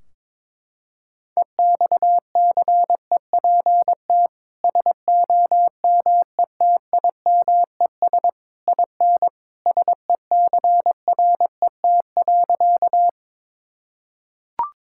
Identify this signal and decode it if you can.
Morse code, heard as a keyed tone, about 22 wpm, 700 Hz. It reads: EXCEPT SOMETIMES IN SECRET.
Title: Sentences from Top 1600 Words 22wpm